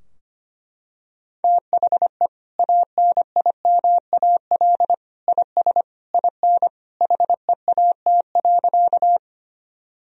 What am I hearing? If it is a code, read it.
Morse code, heard as a keyed tone, about 25 wpm, 700 Hz. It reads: THE ANIMAL IS IN HEAT.
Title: Sentences from Top 1000 Words - Review Part 3 - 25wpm